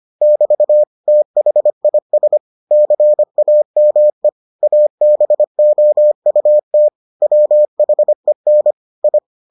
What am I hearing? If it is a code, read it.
Morse code, heard as a keyed tone, about 25 wpm, 600 Hz. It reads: <BT> THIS CAME ABOUT WHEN I